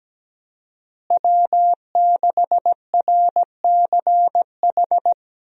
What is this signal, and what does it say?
Morse code, heard as a keyed tone, about 17 wpm, 700 Hz. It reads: W6RCH
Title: USA Call Signs - 5 Characters - Repeated 2x - 17wpm